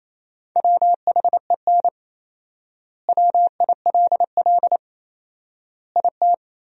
Morse code, heard as a keyed tone, about 28 wpm, 700 Hz. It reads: WHEN WILL IT
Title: 4x Word-Spacing - Sentences from Top 200 Words - 28wpm